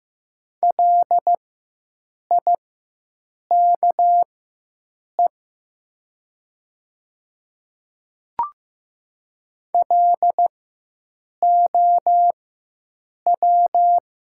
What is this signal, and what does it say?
Morse code, heard as a keyed tone, about 15 wpm, spaced as for 7 wpm, 700 Hz. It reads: LIKE LOW
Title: 4x Character Spacing - Top 200 Words - 15wpm